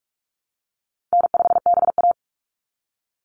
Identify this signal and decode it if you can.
Morse code, heard as a keyed tone, about 44 wpm, 700 Hz. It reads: N5BA
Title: Easy Call Signs Repeated 3x - 45wpm